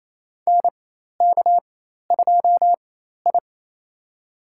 Morse code, heard as a keyed tone, about 28 wpm, spaced as for 13 wpm, 700 Hz. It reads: NK2I
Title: USA Call Signs - 4 Characters - 4x Character Spacing - 28wpm